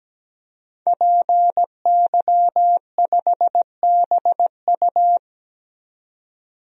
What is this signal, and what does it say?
Morse code, heard as a keyed tone, about 17 wpm, 700 Hz. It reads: PY5BU
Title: DXCC-100 Call Signs - 5 Characters - 17wpm